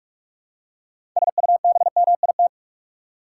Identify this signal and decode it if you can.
Morse code, heard as a keyed tone, about 46 wpm, 700 Hz. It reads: SUBMIT
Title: Six Letter Words - 45wpm